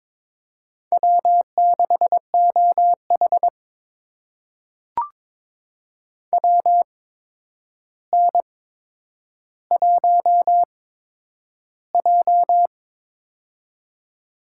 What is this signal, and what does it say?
Morse code, heard as a keyed tone, about 21 wpm, spaced as for 9 wpm, 700 Hz. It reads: W6OH WN1J
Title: USA Call Signs - 4 Characters - 8x Character Spacing - 22wpm